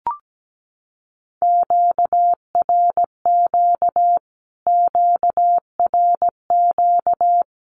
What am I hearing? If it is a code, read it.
Morse code, heard as a keyed tone, about 17 wpm, 700 Hz. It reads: QRQ QRQ
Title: Q-Codes Repeated 3x 17wpm